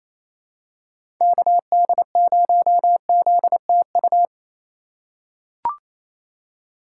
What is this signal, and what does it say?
Morse code, heard as a keyed tone, about 28 wpm, 700 Hz. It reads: KD0ZTU
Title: USA Call Signs - 6 Characters - 3x Character Spacing - 28wpm